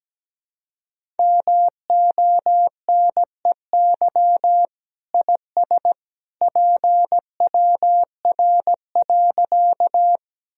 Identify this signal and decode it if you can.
Morse code, heard as a keyed tone, about 17 wpm, 700 Hz. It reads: MONEY IS PWR.
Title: Sentences from Top 500 Words and Abbreviations 17wpm